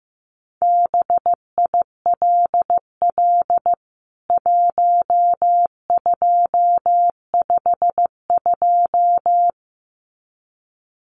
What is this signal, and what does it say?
Morse code, heard as a keyed tone, about 15 wpm, 700 Hz. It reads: BILL 1252
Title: CWT Exchange 15wpm